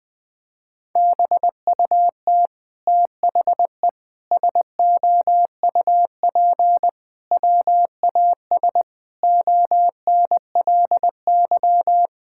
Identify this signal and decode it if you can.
Morse code, heard as a keyed tone, about 20 wpm, 700 Hz. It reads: BUT THE SOUP WAS ONLY
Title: Sentences from Top 1000 Words - Review Part 6 - 20wpm